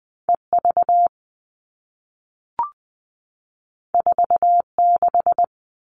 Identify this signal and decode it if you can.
Morse code, heard as a keyed tone, about 20 wpm, 700 Hz. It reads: EV 46
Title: All Cut Numbers - 2-Digit 20wpm